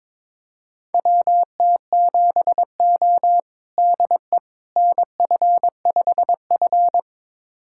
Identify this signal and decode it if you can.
Morse code, heard as a keyed tone, about 22 wpm, 700 Hz. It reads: WT7O DE NF5F
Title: USA Callsign DE Callsign - 4 Characters - 22wpm